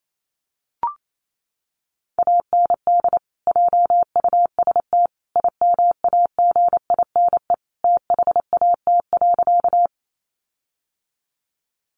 Morse code, heard as a keyed tone, about 28 wpm, 700 Hz. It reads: AND JUST IMAGINE THAT.
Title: Sentences from Top 1400 Words 28wpm